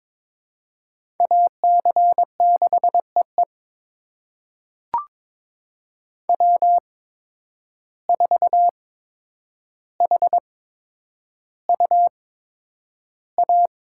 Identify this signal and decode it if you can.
Morse code, heard as a keyed tone, about 22 wpm, spaced as for 9 wpm, 700 Hz. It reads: AC6EE W4HUA
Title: USA Call Signs - 5 Characters - 8x Character Spacing - 22wpm